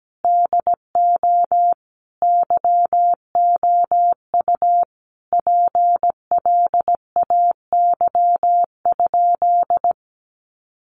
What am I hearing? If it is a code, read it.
Morse code, heard as a keyed tone, about 17 wpm, 700 Hz. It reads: DO YOU PLAY?
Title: Sentences from Top 1400 Words 17wpm